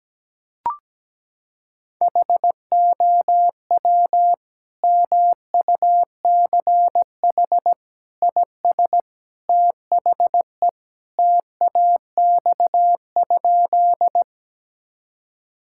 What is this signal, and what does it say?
Morse code, heard as a keyed tone, about 17 wpm, 700 Hz. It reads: HOW MUCH IS THE TAX?
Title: Sentences from Top 1000 Words - Review Part 5 - 17wpm